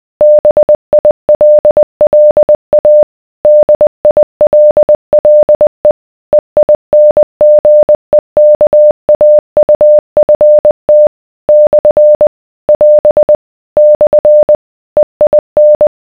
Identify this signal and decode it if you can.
Morse code, heard as a keyed tone, about 20 wpm, 600 Hz. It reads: BILLA DILLE EINGEKAUFT / <AS> / EIN